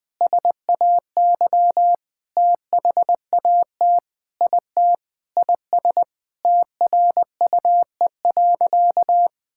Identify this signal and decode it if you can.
Morse code, heard as a keyed tone, about 20 wpm, 700 Hz. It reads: SAY THAT IT IS TRUE.
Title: Sentences with 3-Word Phrases - 20wpm